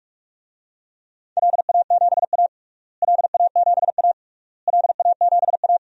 Morse code, heard as a keyed tone, about 47 wpm, 690 Hz. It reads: LA7A LA7A LA7A
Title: DXCC-100 Call Signs - 4 Characters - Repeated 3x - 45wpm